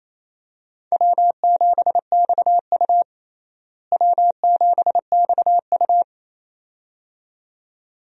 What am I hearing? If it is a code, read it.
Morse code, heard as a keyed tone, about 28 wpm, 700 Hz. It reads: W7XU W7XU
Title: USA Call Signs - 4 Characters - Repeated 2x - 28wpm